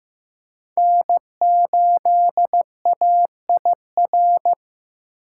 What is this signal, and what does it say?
Morse code, heard as a keyed tone, about 15 wpm, 700 Hz. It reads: N8AIR
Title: USA Call Signs - 5 Characters - 6x Character Spacing - 15wpm